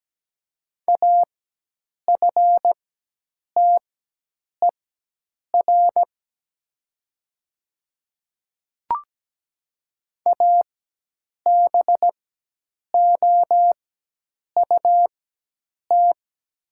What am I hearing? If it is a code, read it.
Morse code, heard as a keyed tone, about 17 wpm, spaced as for 8 wpm, 700 Hz. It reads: AFTER ABOUT
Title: 4x Character Spacing - Top 500 Words - 17wpm